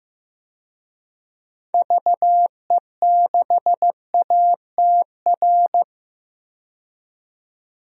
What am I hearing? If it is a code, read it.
Morse code, heard as a keyed tone, about 16 wpm, 700 Hz. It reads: VE6ATR
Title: DXCC-100 Call Signs - 6 Characters - 15wpm